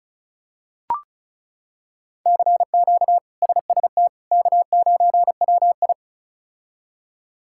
Morse code, heard as a keyed tone, about 36 wpm, 700 Hz. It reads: CQ SST K9WI
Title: K1USN SST QSO Practice - 35wpm